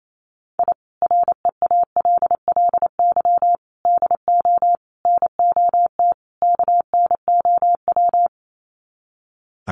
Morse code, heard as a keyed tone, about 28 wpm, 700 Hz. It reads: I REALLY DO NOT KNOW
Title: Sets of 5 Words Encore 28wpm